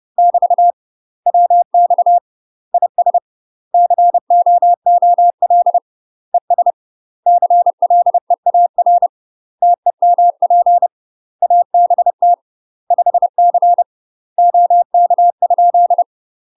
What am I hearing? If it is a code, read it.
Morse code, heard as a keyed tone, about 30 wpm, 700 Hz. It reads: <BT> WX IS COOL ES CLEAR TEMP ABT 5C OK?
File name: UK MASTER QSOs TRAINING / 30WPM / UK MASTER QSOs 0047